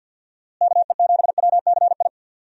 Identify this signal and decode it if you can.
Morse code, heard as a keyed tone, about 52 wpm, 700 Hz. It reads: KE7WCI
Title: USA Call Signs - 6 Characters - Repeated 3x - 50wpm